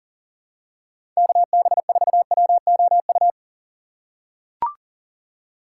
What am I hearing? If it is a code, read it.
Morse code, heard as a keyed tone, about 40 wpm, 700 Hz. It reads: KB4WOU
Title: USA Call Signs - 4x Character Spacing - 40wpm